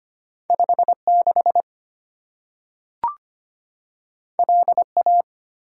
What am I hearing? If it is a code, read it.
Morse code, heard as a keyed tone, about 25 wpm, 690 Hz. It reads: 56 LA
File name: Two Letter-Number Combinations - 25wpm